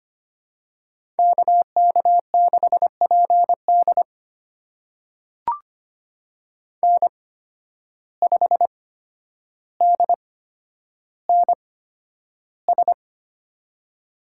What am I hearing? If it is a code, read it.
Morse code, heard as a keyed tone, about 25 wpm, spaced as for 11 wpm, 700 Hz. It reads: KK6PD N5DNS
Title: USA Call Signs - 5 Characters - 8x Character Spacing - 25wpm